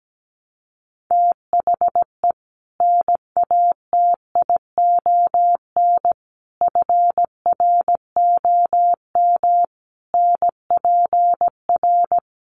Morse code, heard as a keyed tone, about 17 wpm, 700 Hz. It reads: THE NATION FROM NPR
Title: Sets of 5 Words 17wpm